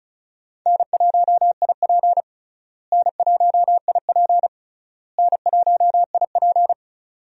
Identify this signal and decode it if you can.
Morse code, heard as a keyed tone, about 36 wpm, 700 Hz. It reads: N1IP N1IP N1IP
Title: USA Call Signs - 4 Characters - Repeated 3x - 35wpm